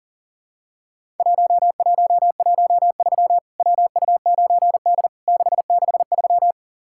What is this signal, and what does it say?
Morse code, heard as a keyed tone, about 44 wpm, 700 Hz. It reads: 1113 WU9D 663
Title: Worked All Europe DX Contest - QTC Exchange - 40wpm